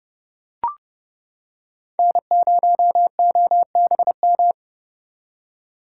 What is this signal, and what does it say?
Morse code, heard as a keyed tone, about 30 wpm, 700 Hz. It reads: N0OBM